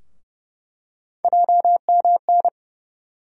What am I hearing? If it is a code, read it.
Morse code, heard as a keyed tone, about 30 wpm, 700 Hz. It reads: JMN
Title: USA Call Sign Postfixes - 3 Letters 30wpm